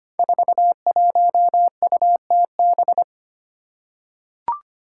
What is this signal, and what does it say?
Morse code, heard as a keyed tone, about 25 wpm, 700 Hz. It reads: 41UTB
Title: Five Challenging Letter-Number Combinations 25wpm